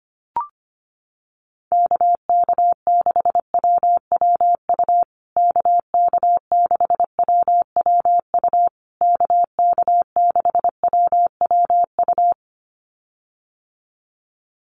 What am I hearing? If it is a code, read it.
Morse code, heard as a keyed tone, about 25 wpm, 700 Hz. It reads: KK6WWU KK6WWU KK6WWU
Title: USA Call Signs Repeated 3x 25wpm